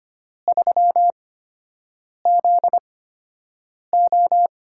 Morse code, heard as a keyed tone, about 25 wpm, spaced as for 7 wpm, 700 Hz. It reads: 3ZO
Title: USA Call Signs - 4 Characters - 8x Character Spacing - 25wpm